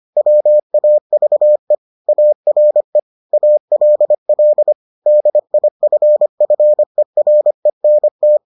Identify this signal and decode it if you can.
Morse code, heard as a keyed tone, about 25 wpm, 600 Hz. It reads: WAVE ARE ALL DIFFERENT